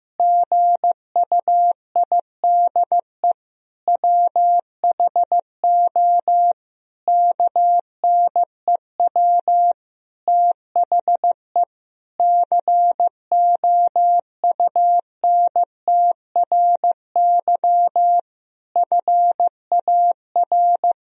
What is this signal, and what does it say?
Morse code, heard as a keyed tone, about 15 wpm, 700 Hz. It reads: GUIDE WHO KNEW THE COUNTRY FAR
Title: Sentences from Top 1000 Words - Review Part 3 - 15wpm